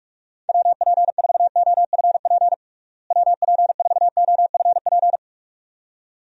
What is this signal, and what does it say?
Morse code, heard as a keyed tone, about 46 wpm, 700 Hz. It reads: WP4OFP WP4OFP
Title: USA Call Signs - 6 Characters - Repeated 2x - 45wpm